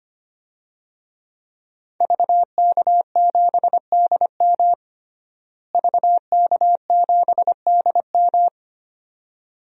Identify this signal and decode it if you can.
Morse code, heard as a keyed tone, about 25 wpm, 700 Hz. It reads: VK7DM VK7DM
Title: DXCC-100 Call Signs - 5 Characters - Repeated 2x - 25wpm